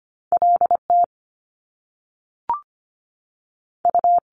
Morse code, heard as a keyed tone, about 25 wpm, 700 Hz. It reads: L T U
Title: Top Two Letter Combinations in Words 25wpm